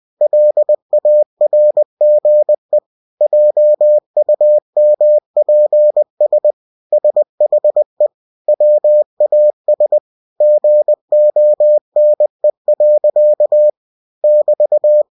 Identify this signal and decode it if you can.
Morse code, heard as a keyed tone, about 20 wpm, 600 Hz. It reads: LARGE JUMPS SHE WAS GONE. <BT>
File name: cw-books / winnie-the-pooh / winnie-20wpm-page-0070